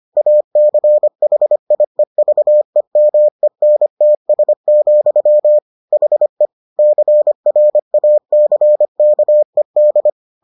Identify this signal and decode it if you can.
Morse code, heard as a keyed tone, about 25 wpm, 600 Hz. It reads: ACHIEVEMENTS, HE CRACKED